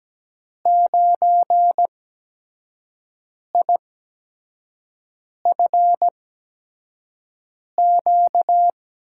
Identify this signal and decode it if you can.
Morse code, heard as a keyed tone, about 17 wpm, spaced as for 5 wpm, 700 Hz. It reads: 9IFQ
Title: USA Call Signs - 8x Character Spacing - 17wpm